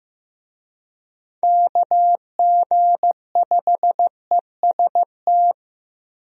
Morse code, heard as a keyed tone, about 16 wpm, 700 Hz. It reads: KG5EST
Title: USA Call Signs - 6 Characters - Repeated 3x - 15wpm